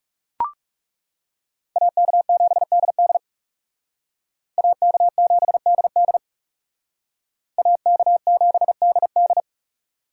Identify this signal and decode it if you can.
Morse code, heard as a keyed tone, about 37 wpm, 700 Hz. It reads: AK7DD AK7DD AK7DD
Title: Speed Racing - USA 2x2 Call Signs 30wpm